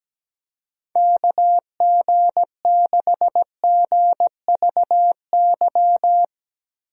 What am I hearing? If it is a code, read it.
Morse code, heard as a keyed tone, about 17 wpm, 700 Hz. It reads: KG6GVY
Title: USA Call Signs - 6 Characters - Repeated 3x - 17wpm